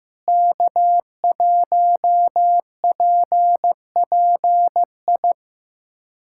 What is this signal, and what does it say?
Morse code, heard as a keyed tone, about 15 wpm, 700 Hz. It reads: K1PPI
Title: USA Callsign DE Callsign - 5 Characters - 15wpm